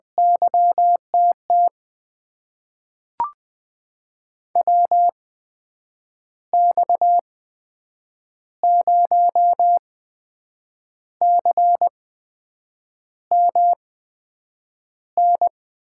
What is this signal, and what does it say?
Morse code, heard as a keyed tone, about 20 wpm, spaced as for 7 wpm, 700 Hz. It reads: YTT WX0CMN
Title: USA Call Signs - 6 Characters - 8x Character Spacing - 20wpm